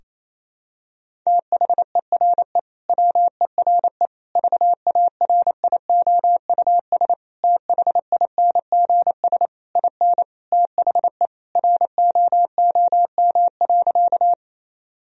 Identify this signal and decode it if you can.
Morse code, heard as a keyed tone, about 28 wpm, 700 Hz. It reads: THERE WERE VARIOUS THINGS IN THE ROOM.
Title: Sentences from Top 1000 Words - Review Part 4 - 28wpm